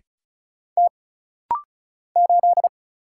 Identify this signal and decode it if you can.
Morse code, heard as a keyed tone, about 36 wpm, 700 Hz. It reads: T 8
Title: Single Letter-Number - Rapid-Fire 35wpm